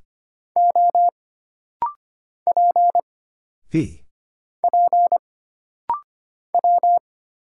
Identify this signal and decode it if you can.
Morse code, heard as a keyed tone, about 25 wpm, 700 Hz. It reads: O P P W
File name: Single Character (Letter:Number:Punctuation:Prosign) - Rapid-Fire - 25wpm